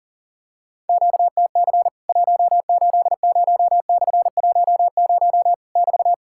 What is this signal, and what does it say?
Morse code, heard as a keyed tone, about 40 wpm, 700 Hz. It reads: QTC 180/10 <BT>
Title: Worked All Europe DX Contest - QTC Exchange - 40wpm